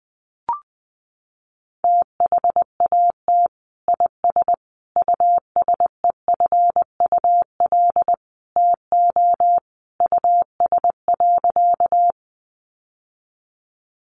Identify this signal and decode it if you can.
Morse code, heard as a keyed tone, about 20 wpm, 700 Hz. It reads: THAT IS USEFUL TO US.
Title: Sentences from Top 700 Words 20wpm